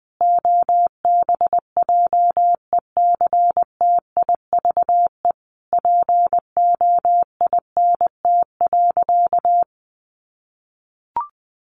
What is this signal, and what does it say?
Morse code, heard as a keyed tone, about 20 wpm, 700 Hz. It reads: OBJECTIVE POINT.
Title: Sentences from Top 1500 Words 20wpm